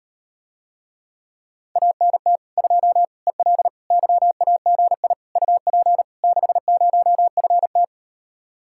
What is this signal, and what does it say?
Morse code, heard as a keyed tone, about 39 wpm, 700 Hz. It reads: ANT 2 EL YAGI UP 60FT